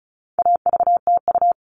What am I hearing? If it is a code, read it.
Morse code, heard as a keyed tone, about 34 wpm, 700 Hz. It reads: AVTU
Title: All Cut Numbers - 4-Digit 35wpm